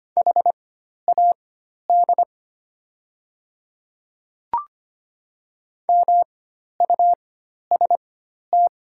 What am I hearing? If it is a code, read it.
Morse code, heard as a keyed tone, about 25 wpm, spaced as for 12 wpm, 700 Hz. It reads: HAD MUST
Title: 4x Character Spacing - Top 500 Words - 25wpm